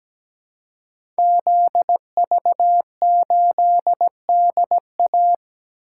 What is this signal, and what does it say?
Morse code, heard as a keyed tone, about 17 wpm, 700 Hz. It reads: ZV8DA
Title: Code Groups - 5 Letter:Numbers - 17wpm